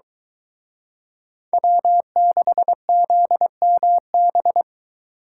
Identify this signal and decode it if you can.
Morse code, heard as a keyed tone, about 23 wpm, 700 Hz. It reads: W6ZMB